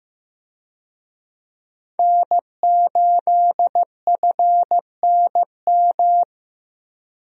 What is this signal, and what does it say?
Morse code, heard as a keyed tone, about 16 wpm, 700 Hz. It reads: N8FNM
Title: USA Call Signs - 5 Characters - Repeated 3x - 15wpm